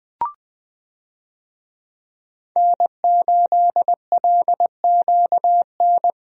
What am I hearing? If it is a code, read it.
Morse code, heard as a keyed tone, about 20 wpm, 700 Hz. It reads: N8LQN
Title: USA Call Signs - 5 Characters - Repeated 2x - 20wpm